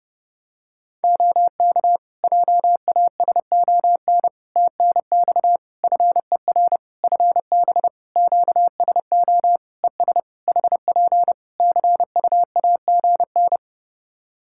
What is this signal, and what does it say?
Morse code, heard as a keyed tone, about 30 wpm, 700 Hz. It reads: OK JASON TNX FER FB QSO ES HP CUAGN
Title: Practice QSO - Exchange 4 - Ending - 30wpm